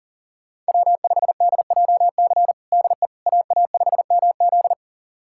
Morse code, heard as a keyed tone, about 40 wpm, 700 Hz. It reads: W5DJC DE AA5MZ
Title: USA Callsign DE Callsign - 5 Characters - 40wpm